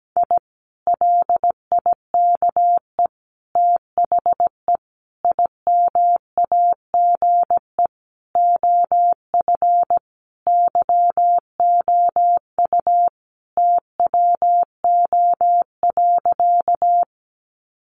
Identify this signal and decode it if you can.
Morse code, heard as a keyed tone, about 17 wpm, 700 Hz. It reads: I LIKE THE IMAGE OF YOU TWO.
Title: Sentences from Top 700 Words 17wpm